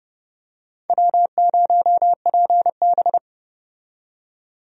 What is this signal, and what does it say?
Morse code, heard as a keyed tone, about 32 wpm, 700 Hz. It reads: W0PB
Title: USA Call Signs - 4 Characters - 30wpm